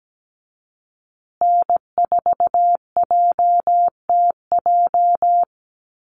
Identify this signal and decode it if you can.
Morse code, heard as a keyed tone, about 17 wpm, 700 Hz. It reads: N4JTJ
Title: USA Call Signs Repeated 3x 17wpm